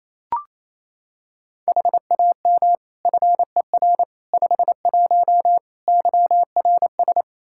Morse code, heard as a keyed tone, about 27 wpm, 700 Hz. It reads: HAM FER 51 YRS
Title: Practice QSO - Exchange 3 - More Info - 28wpm